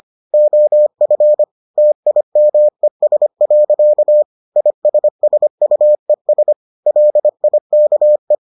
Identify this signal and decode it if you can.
Morse code, heard as a keyed tone, about 25 wpm, 600 Hz. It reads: OF TIMES. ISSUES LIKE